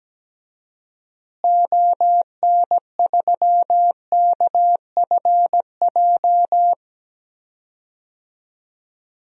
Morse code, heard as a keyed tone, about 17 wpm, 700 Hz. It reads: ON3KFJ
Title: DXCC-100 Call Signs - 6 Characters - Repeated 3x - 17wpm